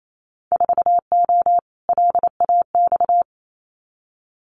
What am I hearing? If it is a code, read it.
Morse code, heard as a keyed tone, about 28 wpm, 700 Hz. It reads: 4O LAX
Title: Winter Field Day Exchanges Encore 28wpm